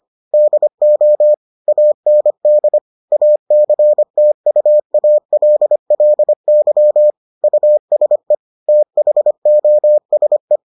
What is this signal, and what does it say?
Morse code, heard as a keyed tone, about 25 wpm, 600 Hz. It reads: DO AND ACTUALLY USE THOSE